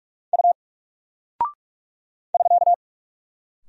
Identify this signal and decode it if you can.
Morse code, heard as a keyed tone, about 46 wpm, 700 Hz. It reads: U <SK>
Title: Single Character (Letter:Number:Punctuation:Prosign) Alt 45wpm